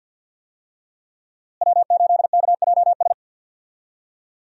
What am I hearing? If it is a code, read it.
Morse code, heard as a keyed tone, about 47 wpm, 700 Hz. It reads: W8KJS
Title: USA Call Signs - 5 Characters - Repeated 3x - 50wpm